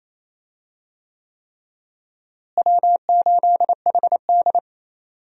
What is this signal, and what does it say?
Morse code, heard as a keyed tone, about 28 wpm, 700 Hz. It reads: W8HD
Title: USA Call Signs - 4 Characters - Repeated 3x - 28wpm